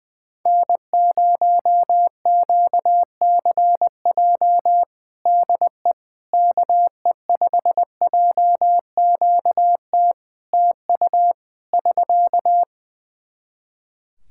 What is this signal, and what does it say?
Morse code, heard as a keyed tone, about 20 wpm, 700 Hz. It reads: N0QCJ DE KE5JQT TU <SK>
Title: Practice QSO - Exchange 4 - Ending - 20wpm